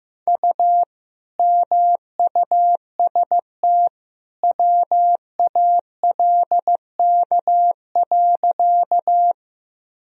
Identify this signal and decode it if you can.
Morse code, heard as a keyed tone, about 15 wpm, 700 Hz. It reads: U MUST WALK.
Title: Sentences from Top 300 Words and Abbreviations 15wpm